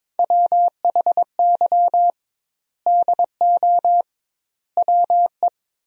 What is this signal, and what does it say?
Morse code, heard as a keyed tone, about 23 wpm, 700 Hz. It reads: WHY DO WE
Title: 2x Word-Spacing - Sentences from Top 400 Words - 22wpm